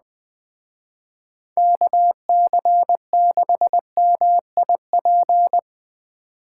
Morse code, heard as a keyed tone, about 21 wpm, 700 Hz. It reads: KC6MIP